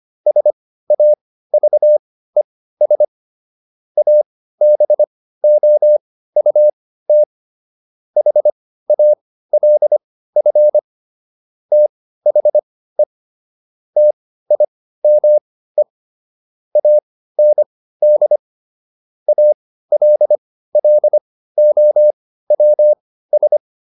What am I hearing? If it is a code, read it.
Morse code, heard as a keyed tone, about 25 wpm, spaced as for 15 wpm, 600 Hz. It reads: SAVES ABOUT HALF THE TIME AND ALLOWS